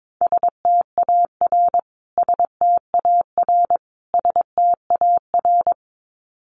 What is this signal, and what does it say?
Morse code, heard as a keyed tone, about 22 wpm, 700 Hz. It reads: STAR STAR STAR
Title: Top 500 Words Repeated 22wpm